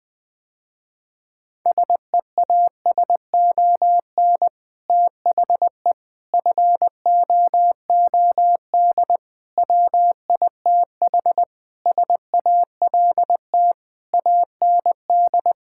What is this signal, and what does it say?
Morse code, heard as a keyed tone, about 20 wpm, 700 Hz. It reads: SEASON THE FOOD WITH SALT AND
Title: Sentences with 5-Word Phrases - Spoken Beginning - 20wpm